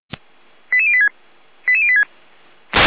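Five-tone selective call (ZVEI), heard as quick runs of five tones, tones of about 70 ms. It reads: 88086 88086
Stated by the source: Five-tone ZVEI selective call received on an SDR.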